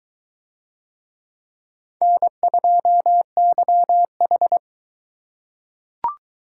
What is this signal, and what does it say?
Morse code, heard as a keyed tone, about 22 wpm, 700 Hz. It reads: N2YH